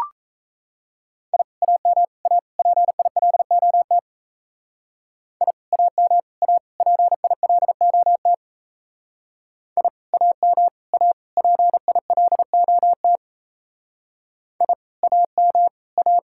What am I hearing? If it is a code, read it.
Morse code, heard as a keyed tone, about 34 wpm, 700 Hz. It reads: I AM A PILOT I AM A PILOT I AM A PILOT I AM A
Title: Speed-Racing Common QSO Phrases - 28wpm